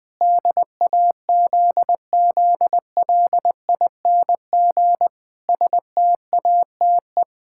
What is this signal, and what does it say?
Morse code, heard as a keyed tone, about 20 wpm, 700 Hz. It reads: DAZZLING STATE
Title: Pangram Challenge v2 - 20wpm